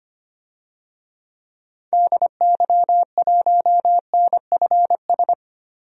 Morse code, heard as a keyed tone, about 25 wpm, 700 Hz. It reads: DY1NFS
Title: DXCC-100 Call Signs - 6 Characters - 25wpm